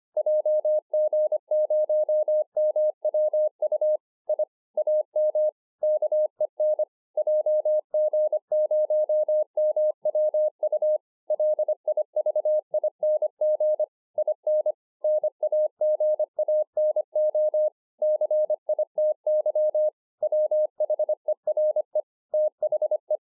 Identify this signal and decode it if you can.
Morse code, heard as a keyed tone, about 25 wpm, 610 Hz. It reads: JG0MWU I AM KEN JG0MWU LIVING IN NAGANO CITY WHERE THE